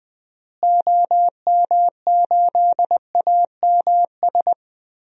Morse code, heard as a keyed tone, about 20 wpm, 700 Hz. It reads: OM8AMS
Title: DXCC-100 Call Signs - 6 Characters - Repeated 3x - 20wpm